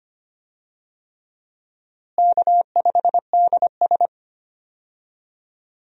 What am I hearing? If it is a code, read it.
Morse code, heard as a keyed tone, about 25 wpm, 700 Hz. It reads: K5DS
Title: USA Call Signs - 4 Characters - Repeated 3x - 25wpm